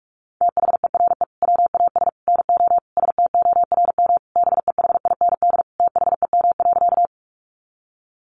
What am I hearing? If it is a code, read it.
Morse code, heard as a keyed tone, about 44 wpm, 700 Hz. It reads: THERE WAS NO STORM BEHIND THEM.
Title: Sentences from Top 1500 Words 45wpm